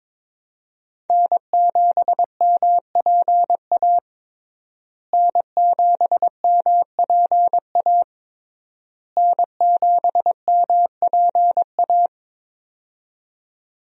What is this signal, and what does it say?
Morse code, heard as a keyed tone, about 22 wpm, 700 Hz. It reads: N7MPA N7MPA N7MPA
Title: USA Call Signs - 5 Characters - Repeated 3x - 22wpm